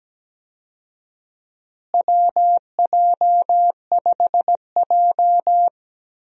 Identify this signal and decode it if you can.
Morse code, heard as a keyed tone, about 17 wpm, 700 Hz. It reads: WJ5J
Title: USA Call Signs - 4 Characters - Repeated 2x - 17wpm